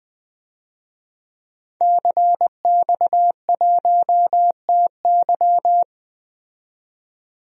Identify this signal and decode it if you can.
Morse code, heard as a keyed tone, about 20 wpm, 700 Hz. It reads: CX1TY
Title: DXCC-100 Call Signs - 5 Characters - 20wpm